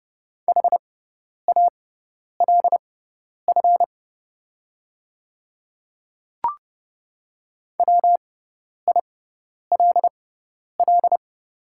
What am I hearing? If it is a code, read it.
Morse code, heard as a keyed tone, about 30 wpm, spaced as for 10 wpm, 700 Hz. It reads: HALF WILL
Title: 6x Character Spacing - Top 500 Words - 30wpm